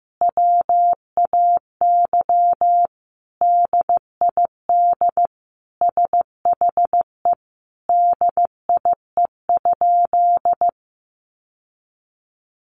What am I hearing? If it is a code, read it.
Morse code, heard as a keyed tone, about 15 wpm, 700 Hz. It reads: WAY DID SHE DIE?
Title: Sentences from Top 1300 Words 15wpm